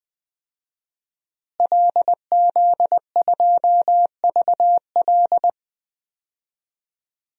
Morse code, heard as a keyed tone, about 21 wpm, 700 Hz. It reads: LZ2VL
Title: DXCC-100 Call Signs - 5 Characters - Repeated 3x - 20wpm